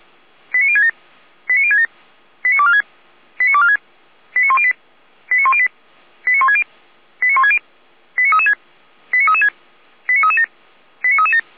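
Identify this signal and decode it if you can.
Five-tone selective call (ZVEI), heard as quick runs of five tones, tones of about 70 ms. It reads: 89067 89067 89246 89246 89108 89108 89160 89160 89336 89336 89337 89337
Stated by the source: ZVEI five-tone call received on an SDR.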